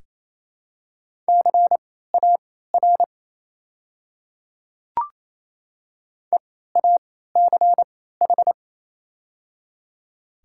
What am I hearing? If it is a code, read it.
Morse code, heard as a keyed tone, about 28 wpm, spaced as for 16 wpm, 700 Hz. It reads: CAR EACH
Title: 3x Character Spacing - Top 300 Words - 28wpm